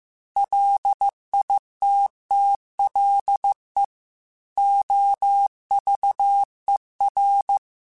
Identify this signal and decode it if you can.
Morse code, heard as a keyed tone, about 15 wpm, 800 Hz. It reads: LITTLE OVER